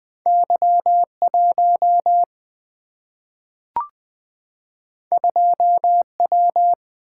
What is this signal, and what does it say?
Morse code, heard as a keyed tone, about 20 wpm, 700 Hz. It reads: Y 1 2 W
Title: Two Letter-Number Combinations - 20wpm